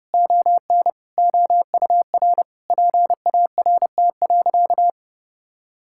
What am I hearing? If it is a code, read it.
Morse code, heard as a keyed tone, about 30 wpm, 700 Hz. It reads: ON OUR PART.
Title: Sentences from Top 1000 Words - Review Part 6 - 30wpm